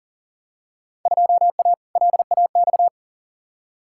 Morse code, heard as a keyed tone, about 40 wpm, 700 Hz. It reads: 2A LAX
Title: Field Day Exchanges - 40wpm